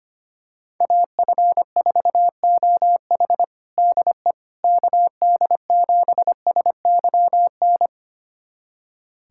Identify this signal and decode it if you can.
Morse code, heard as a keyed tone, about 25 wpm, 700 Hz. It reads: AF4OH DE KD7SYN
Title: USA Callsign DE Callsign 25wpm